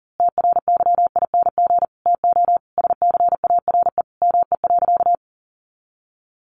Morse code, heard as a keyed tone, about 40 wpm, 700 Hz. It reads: TRYING TO SCARE ME.
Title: A Sound Of Thunder 40wpm